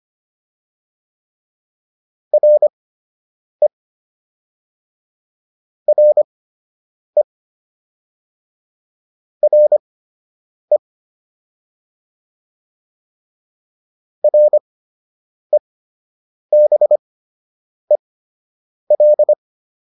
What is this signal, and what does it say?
Morse code, heard as a keyed tone, about 25 wpm, spaced as for 8 wpm, 600 Hz. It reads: RE RE RE REBEL